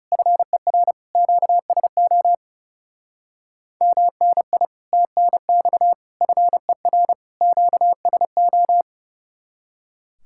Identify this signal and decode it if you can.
Morse code, heard as a keyed tone, about 36 wpm, 700 Hz. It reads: FER QSO MNI TNX FER QSO
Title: Speed-Racing Common QSO Phrases - 30wpm